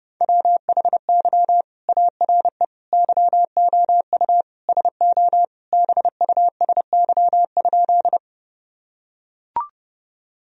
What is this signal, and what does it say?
Morse code, heard as a keyed tone, about 30 wpm, 700 Hz. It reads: WHY ARE YOU SO BUSY?
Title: Sentences from Top 1000 Words - Review Part 3 - 30wpm